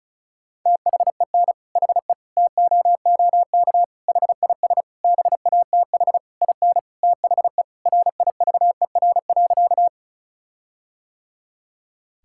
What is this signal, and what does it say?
Morse code, heard as a keyed tone, about 36 wpm, 700 Hz. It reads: THEN HE TOOK HIS BATH IN THE RIVER.
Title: Sentences from Top 1600 Words 35wpm